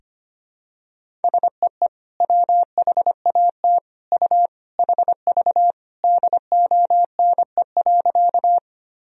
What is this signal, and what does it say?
Morse code, heard as a keyed tone, about 25 wpm, 700 Hz. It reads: SEE WHAT U HV DONE.
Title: Sentences from Top 500 Words and Abbreviations 25wpm